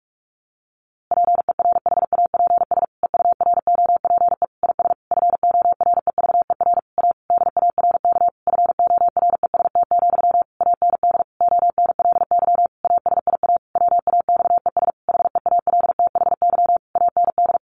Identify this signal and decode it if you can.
Morse code, heard as a keyed tone, about 44 wpm, 700 Hz. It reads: PERHAPS EUROPE IS FOREVER A DARK FOREST, AND ONLY ASIA WAXES HEALTHY AND
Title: A Sound Of Thunder 45wpm